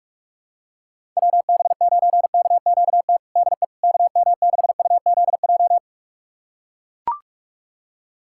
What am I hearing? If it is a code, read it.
Morse code, heard as a keyed tone, about 43 wpm, 700 Hz. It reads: WB9KQT DE KM6UZJ
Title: USA Callsign DE Callsign - 6 Characters - 45wpm